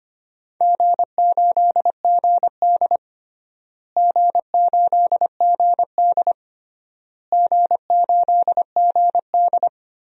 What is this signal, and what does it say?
Morse code, heard as a keyed tone, about 25 wpm, 700 Hz. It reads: G8GD G8GD G8GD
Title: DXCC-100 Call Signs - 4 Characters - Repeated 3x - 25wpm